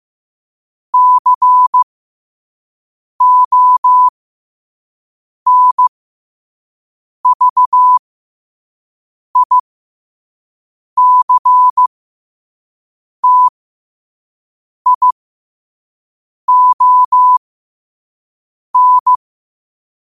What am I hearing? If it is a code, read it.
Morse code, heard as a keyed tone, about 15 wpm, spaced as for 5 wpm, 1000 Hz. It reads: CONVICTION
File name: qotdfiles / 5wpm